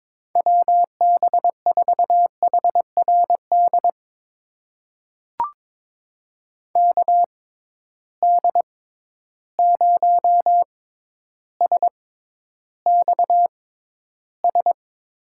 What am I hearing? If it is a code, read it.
Morse code, heard as a keyed tone, about 22 wpm, spaced as for 11 wpm, 700 Hz. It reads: WB4HRD KD0SXS
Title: USA Call Signs - 6 Characters - 6x Character Spacing - 22wpm